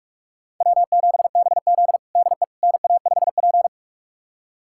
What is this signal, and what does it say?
Morse code, heard as a keyed tone, about 42 wpm, 690 Hz. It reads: W7BZ DE NA5P
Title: USA Callsign DE Callsign - 4 Characters - 45wpm